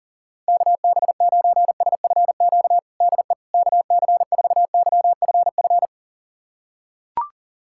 Morse code, heard as a keyed tone, about 40 wpm, 700 Hz. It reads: KB9SFQ DE KC4YFF
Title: USA Callsign DE Callsign - 6 Characters - 40wpm